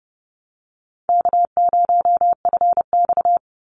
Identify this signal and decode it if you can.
Morse code, heard as a keyed tone, about 31 wpm, 700 Hz. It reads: K0FX
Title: Call Signs - 30wpm